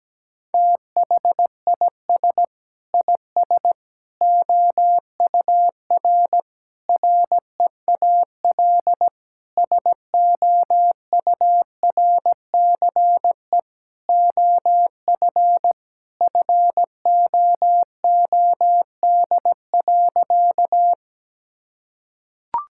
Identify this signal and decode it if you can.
Morse code, heard as a keyed tone, about 17 wpm, 700 Hz. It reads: THIS IS OUR REAL SOURCE OF FOOD.
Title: Sentences from Top 1000 Words - Review Part 5 - 17wpm